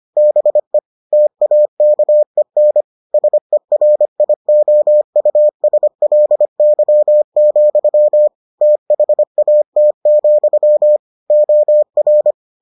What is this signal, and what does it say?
Morse code, heard as a keyed tone, about 25 wpm, 600 Hz. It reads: BE TAKEN SERIOUSLY, THAT, OR